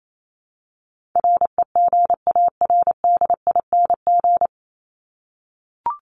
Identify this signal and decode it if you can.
Morse code, heard as a keyed tone, about 28 wpm, 700 Hz. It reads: REGARDING
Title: ING Suffix - Top 100 Words with Base Words 28wpm